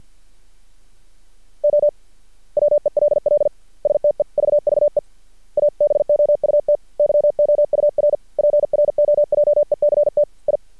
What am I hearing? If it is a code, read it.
Morse code, heard as a keyed tone, about 51 wpm, 590 Hz. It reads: O WELL STEVVE ABOUT YOUR PROJECT I
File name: archived / CRSnet